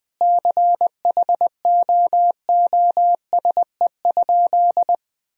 Morse code, heard as a keyed tone, about 20 wpm, 700 Hz. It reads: CHOOSE?
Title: Sentences from Top 1000 Words - Review Part 4 - 20wpm